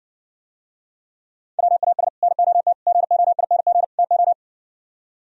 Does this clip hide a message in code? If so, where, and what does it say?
Yes: Morse code from 1.6 to 4.3 s, heard as a keyed tone, about 60 wpm, 700 Hz. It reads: WAS NOT GOING TO